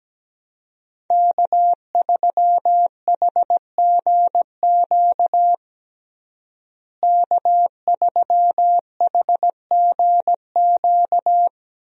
Morse code, heard as a keyed tone, about 17 wpm, 700 Hz. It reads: K3HGQ K3HGQ
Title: USA Call Signs - 5 Characters - Repeated 3x - 17wpm